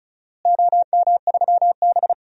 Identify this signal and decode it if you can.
Morse code, heard as a keyed tone, about 36 wpm, 700 Hz. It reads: OM3B
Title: DXCC-100 Call Signs - 4 Characters - Repeated 3x - 35wpm